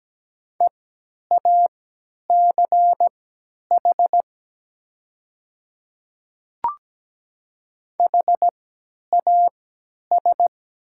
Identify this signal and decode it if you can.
Morse code, heard as a keyed tone, about 17 wpm, spaced as for 10 wpm, 700 Hz. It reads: EACH HAS
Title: 3x Character Spacing - Top 100 Words - 17wpm